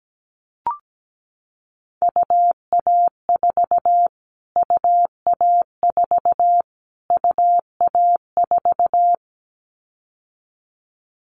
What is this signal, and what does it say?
Morse code, heard as a keyed tone, about 17 wpm, 700 Hz. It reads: UA4 UA4 UA4
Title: Callsign Prefixes for Top 25 DXCC - Repeated 3x - 17wpm